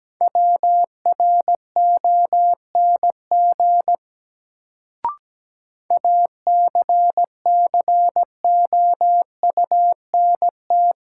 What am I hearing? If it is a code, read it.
Morse code, heard as a keyed tone, about 17 wpm, 700 Hz. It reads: WRONG ACCOUNT
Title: Sentences from Top 900 Words - New Word Drill - 17wpm